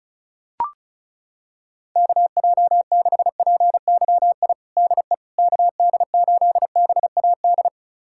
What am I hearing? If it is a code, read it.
Morse code, heard as a keyed tone, about 34 wpm, 700 Hz. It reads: KJ6PYI DE KD8BAD
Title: USA Callsign DE Callsign - 6 Characters - 35wpm